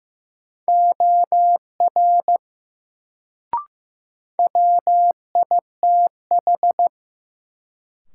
Morse code, heard as a keyed tone, about 15 wpm, 700 Hz. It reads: OR WITH
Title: Sentences from Top 100 Words - New Word Drill - 15wpm